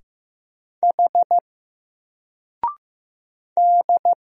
Morse code, heard as a keyed tone, about 15 wpm, 700 Hz. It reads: H D
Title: Single Letter-Number Alt 15wpm